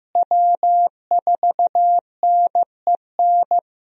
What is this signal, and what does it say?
Morse code, heard as a keyed tone, about 15 wpm, 700 Hz. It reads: W4NEN
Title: USA Call Signs - 5 Characters - Repeated 2x - 15wpm